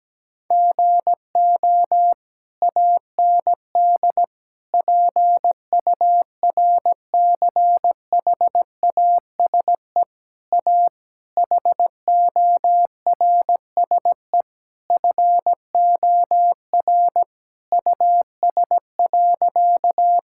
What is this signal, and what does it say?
Morse code, heard as a keyed tone, about 17 wpm, 700 Hz. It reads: GO AND PURCHASE A HORSE FOR US.
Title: Sentences from Top 1000 Words - Review Part 5 - 17wpm